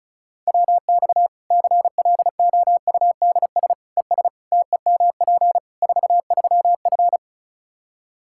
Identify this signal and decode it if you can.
Morse code, heard as a keyed tone, about 34 wpm, 700 Hz. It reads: WX CLOUDS ES TEMP 43F
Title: K1USN SST QSO Practice 35wpm